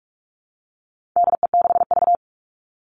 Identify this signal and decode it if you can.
Morse code, heard as a keyed tone, about 44 wpm, 700 Hz. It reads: DE6V
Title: All Cut Numbers - 4-Digit 45wpm